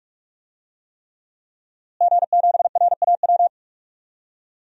Morse code, heard as a keyed tone, about 46 wpm, 700 Hz. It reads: G7RAW
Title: DXCC-100 Call Signs - 5 Characters - Repeated 3x - 45wpm